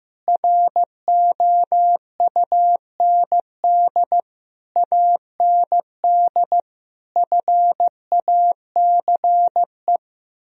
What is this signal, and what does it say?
Morse code, heard as a keyed tone, about 15 wpm, 700 Hz. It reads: ROUND AND FACE
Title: Sentences from Top 1000 Words - Review Part 2 - 15wpm